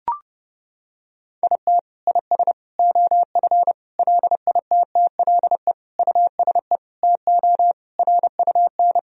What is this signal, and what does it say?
Morse code, heard as a keyed tone, about 32 wpm, 700 Hz. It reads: IT IS OF LITTLE USE TO RUN
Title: Sentences from Top 1000 Words - Review Part 1 - 30wpm